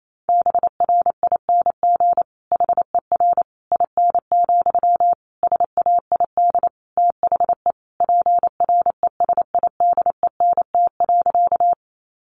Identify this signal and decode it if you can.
Morse code, heard as a keyed tone, about 28 wpm, 700 Hz. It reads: BRING HER IN, SAID THE PRESIDENT.
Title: Sentences from Top 1000 Words 28wpm